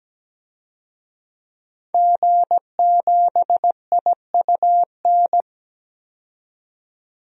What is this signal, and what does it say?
Morse code, heard as a keyed tone, about 17 wpm, 700 Hz. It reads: G7IUN
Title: DXCC-100 Call Signs - 5 Characters - 17wpm